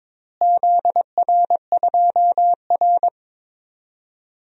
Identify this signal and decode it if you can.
Morse code, heard as a keyed tone, about 22 wpm, 700 Hz. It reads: ZR2R
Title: DXCC-100 Call Signs - 4 Characters - Repeated 3x - 22wpm